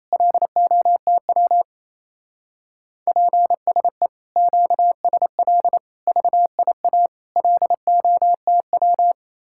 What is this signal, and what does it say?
Morse code, heard as a keyed tone, about 35 wpm, 700 Hz. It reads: LOTW PSE QSL VIA LOTW
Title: Speed-Racing Common QSO Phrases Encore - 28wpm